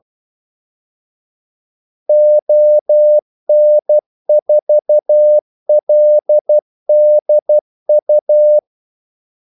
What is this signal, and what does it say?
Morse code, heard as a keyed tone, about 12 wpm, 600 Hz. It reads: ON4LDU
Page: 2 ekstra ordmellemrum mellem kaldesignalerne